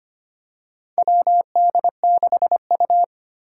Morse code, heard as a keyed tone, about 25 wpm, 700 Hz. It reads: WD6U
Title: USA Call Signs - 4 Characters - 2x Character Spacing - 25wpm